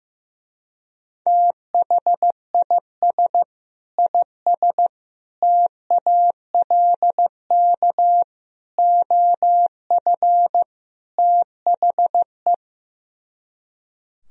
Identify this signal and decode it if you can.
Morse code, heard as a keyed tone, about 15 wpm, 700 Hz. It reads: THIS IS TALK OF THE
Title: Speed-Racing - Sets of 5 Words 15wpm